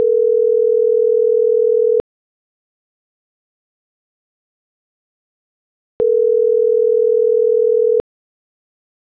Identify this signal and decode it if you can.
Ringback tone (North America), 440 + 480 Hz.